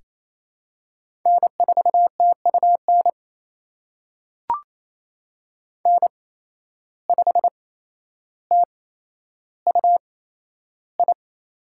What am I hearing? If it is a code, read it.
Morse code, heard as a keyed tone, about 28 wpm, spaced as for 12 wpm, 700 Hz. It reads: N4TUN N5TUI
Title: USA Call Signs - 5 Characters - 8x Character Spacing - 28wpm